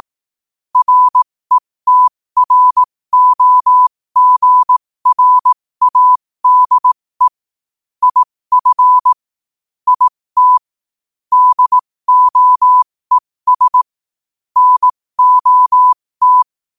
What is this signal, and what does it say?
Morse code, heard as a keyed tone, about 18 wpm, spaced as for 15 wpm, 1000 Hz. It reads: RETROGRADE IF IT DOES NOT
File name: qotdfiles / 15wpm